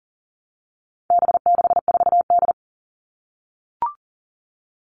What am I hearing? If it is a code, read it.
Morse code, heard as a keyed tone, about 40 wpm, 700 Hz. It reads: B64D
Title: All Cut Numbers - 4-Digit 40wpm